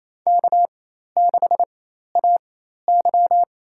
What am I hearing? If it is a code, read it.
Morse code, heard as a keyed tone, about 27 wpm, spaced as for 13 wpm, 700 Hz. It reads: K6AY
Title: USA Call Signs - 4 Characters - 4x Character Spacing - 28wpm